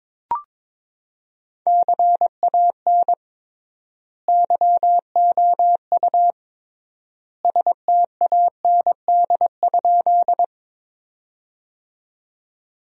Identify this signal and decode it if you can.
Morse code, heard as a keyed tone, about 22 wpm, 700 Hz. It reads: CAN YOU STAND?
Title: 3x Word-Spacing - Sentences from Top 300 Words - 22wpm